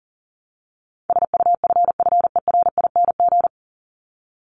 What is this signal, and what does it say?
Morse code, heard as a keyed tone, about 40 wpm, 700 Hz. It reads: SUFFERING
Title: ING Suffix - Top 100 Words with Base Words 40wpm